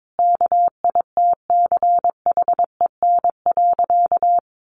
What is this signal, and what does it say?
Morse code, heard as a keyed tone, about 22 wpm, 700 Hz. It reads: KITCHEN.
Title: Sentences from Top 1400 Words 22wpm